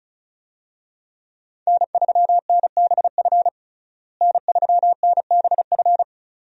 Morse code, heard as a keyed tone, about 36 wpm, 700 Hz. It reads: N3NBF N3NBF
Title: USA Call Signs - 5 Characters - Repeated 2x - 35wpm